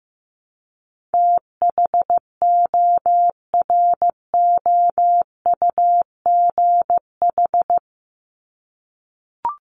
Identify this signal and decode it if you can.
Morse code, heard as a keyed tone, about 15 wpm, 700 Hz. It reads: THOROUGH
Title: LY Suffix - Top 100 Words with Base Words 15wpm